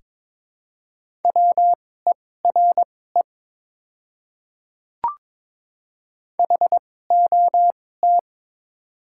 Morse code, heard as a keyed tone, about 22 wpm, spaced as for 16 wpm, 700 Hz. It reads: WERE HOT
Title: 2x Character Spacing - Top 500 Words - 22wpm